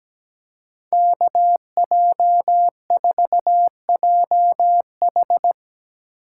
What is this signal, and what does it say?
Morse code, heard as a keyed tone, about 18 wpm, 700 Hz. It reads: KJ4JH
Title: Speed Racing - USA 2x2 Call Signs 17wpm